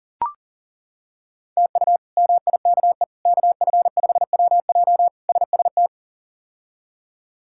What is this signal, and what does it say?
Morse code, heard as a keyed tone, about 40 wpm, 700 Hz. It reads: TU MIKE KF5WJ SST
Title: K1USN SST QSO Practice - 40wpm